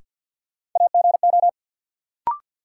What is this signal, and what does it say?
Morse code, heard as a keyed tone, about 52 wpm, 700 Hz. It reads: AGO
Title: Sets of 2 Words - New Word Drill - 50wpm